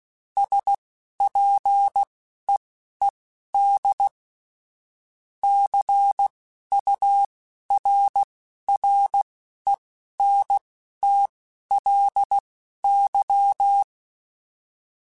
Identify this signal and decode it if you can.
Morse code, heard as a keyed tone, about 16 wpm, spaced as for 12 wpm, 800 Hz. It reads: SPEED CURRENTLY
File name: Weather Sets / Weather Sets-mp3s / Weather 5 12 @10